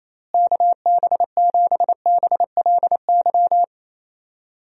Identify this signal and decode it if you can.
Morse code, heard as a keyed tone, about 28 wpm, 700 Hz. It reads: KB7BLY
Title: USA Call Signs - 6 Characters - Repeated 3x - 28wpm